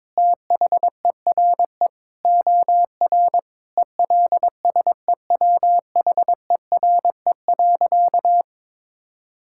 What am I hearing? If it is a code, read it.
Morse code, heard as a keyed tone, about 22 wpm, 700 Hz. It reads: THERE OR ELSEWHERE.
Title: Sentences from Top 1600 Words 22wpm